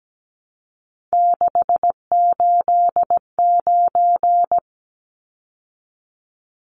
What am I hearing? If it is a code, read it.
Morse code, heard as a keyed tone, about 17 wpm, 700 Hz. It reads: 689
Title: Numbers - 3-Digit 17wpm